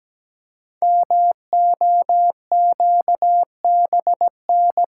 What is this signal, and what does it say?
Morse code, heard as a keyed tone, about 17 wpm, 700 Hz. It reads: MOQBN
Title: Code Groups - 5 Letters - 17wpm